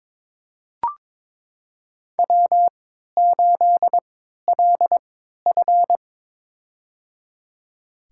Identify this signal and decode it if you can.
Morse code, heard as a keyed tone, about 22 wpm, spaced as for 13 wpm, 700 Hz. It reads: W8LF
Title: USA Call Signs - 4 Characters - 3x Character Spacing - 22wpm